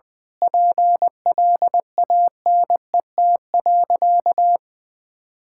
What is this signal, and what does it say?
Morse code, heard as a keyed tone, about 20 wpm, 700 Hz. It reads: PLANET.
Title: Difficult Sentences with Hints - 20wpm